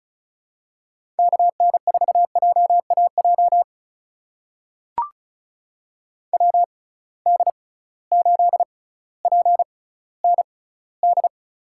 Morse code, heard as a keyed tone, about 34 wpm, spaced as for 18 wpm, 700 Hz. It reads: KN4JAJ WD8PND
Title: USA Call Signs - 6 Characters - 6x Character Spacing - 35wpm